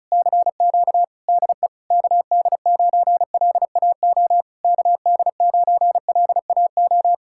Morse code, heard as a keyed tone, about 40 wpm, 700 Hz. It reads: CQ DE KD9LAO KD9LAO